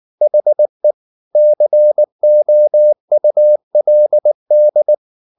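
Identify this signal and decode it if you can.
Morse code, heard as a keyed tone, about 19 wpm, 600 Hz. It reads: HE COULD
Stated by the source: Morse code at 19 wpm from CW med Gnister, reading Ken Follett's novel Never in English.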